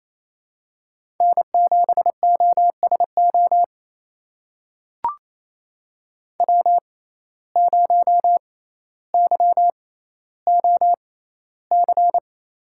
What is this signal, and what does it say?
Morse code, heard as a keyed tone, about 27 wpm, spaced as for 14 wpm, 700 Hz. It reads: N7OSO W0YOC
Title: USA Call Signs - 5 Characters - 6x Character Spacing - 28wpm